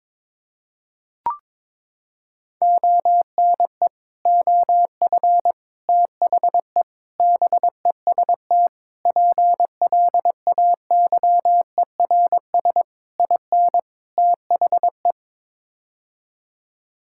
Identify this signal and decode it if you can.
Morse code, heard as a keyed tone, about 22 wpm, 700 Hz. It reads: ONE OF THE BEST PLAYERS IN THE
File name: practice / 22wpm / Sets of 7 Words - 22wpm